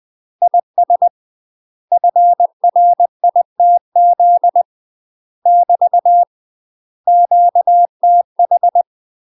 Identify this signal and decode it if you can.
Morse code, heard as a keyed tone, about 20 wpm, 700 Hz. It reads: IS FRITZ <BT> QTH
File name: UK MASTER QSOs TRAINING/20WPM